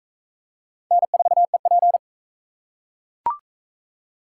Morse code, heard as a keyed tone, about 37 wpm, 700 Hz. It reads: N4EP